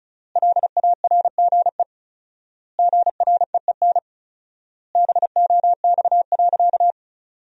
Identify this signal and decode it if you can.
Morse code, heard as a keyed tone, about 36 wpm, 700 Hz. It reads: LARGE GREEN BOX.
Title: 4x Word-Spacing - Sentences from Top 500 Words - 35wpm